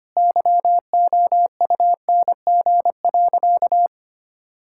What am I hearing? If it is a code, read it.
Morse code, heard as a keyed tone, about 25 wpm, 700 Hz. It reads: YOUNG.
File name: practice / 25wpm / Sentences with 3-Word Phrases - 25wpm